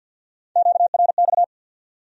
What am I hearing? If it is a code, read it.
Morse code, heard as a keyed tone, about 62 wpm, 700 Hz. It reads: QRX
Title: Speed-Racing Q-Codes - 50wpm